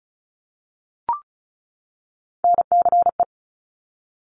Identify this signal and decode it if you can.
Morse code, heard as a keyed tone, about 36 wpm, 700 Hz. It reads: NCE
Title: Top Three Letter Combinations in Words 35wpm